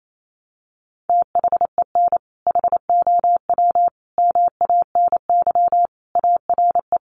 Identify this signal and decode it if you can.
Morse code, heard as a keyed tone, about 28 wpm, 700 Hz. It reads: THEN HOW MANY ARE
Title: Sentences from Top 100 Words 28wpm